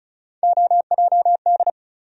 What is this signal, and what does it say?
Morse code, heard as a keyed tone, about 34 wpm, 700 Hz. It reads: OJD
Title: USA Call Sign Postfixes - 3 Letters 35wpm